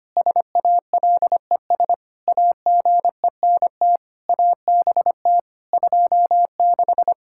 Morse code, heard as a keyed tone, about 25 wpm, 700 Hz. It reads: SALES AGENT ABT 26
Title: Practice QSO - Exchange 3 - More Info - 25wpm